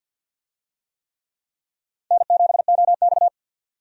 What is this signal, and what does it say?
Morse code, heard as a keyed tone, about 53 wpm, 700 Hz. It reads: N7OX
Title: USA Call Signs - 4 Characters - 50wpm